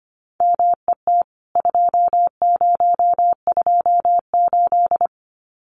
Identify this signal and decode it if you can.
Morse code, heard as a keyed tone, about 25 wpm, 700 Hz. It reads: MET 2028
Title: CWT Exchange 25wpm